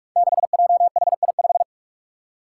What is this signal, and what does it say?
Morse code, heard as a keyed tone, about 46 wpm, 700 Hz. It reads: 6JHI5
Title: Five Challenging Letter-Number Combinations 45wpm